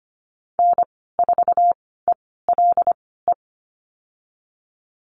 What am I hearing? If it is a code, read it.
Morse code, heard as a keyed tone, about 25 wpm, spaced as for 16 wpm, 700 Hz. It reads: N4ELE
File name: Lesson 15g - Number 5 - Call Signs 25wpm